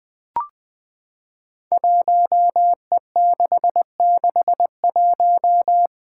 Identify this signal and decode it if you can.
Morse code, heard as a keyed tone, about 20 wpm, 700 Hz. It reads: 1E661
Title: Five Challenging Letter-Number Combinations 20wpm